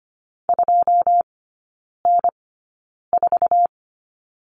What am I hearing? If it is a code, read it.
Morse code, heard as a keyed tone, about 25 wpm, spaced as for 9 wpm, 700 Hz. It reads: 2N4
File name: Lesson 14e - Number 2 - Three Characters 25wpm